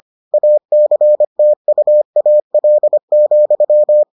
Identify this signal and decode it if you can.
Morse code, heard as a keyed tone, about 25 wpm, 600 Hz. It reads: ACTUAL,